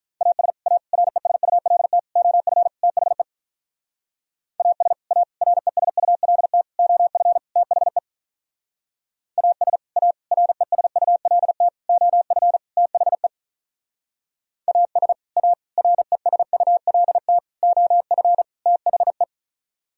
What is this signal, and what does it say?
Morse code, heard as a keyed tone, about 46 wpm, 700 Hz. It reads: AS A RESULT OF THE AS A RESULT OF THE AS A RESULT OF THE AS A RESULT OF THE